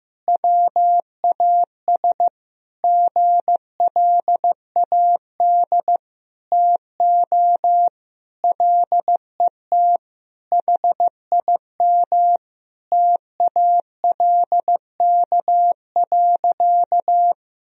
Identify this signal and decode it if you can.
Morse code, heard as a keyed tone, about 16 wpm, 700 Hz. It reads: WAS GLAD TO LET HIM TALK.
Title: Sentences from Top 1000 Words - Review Part 4 - 15wpm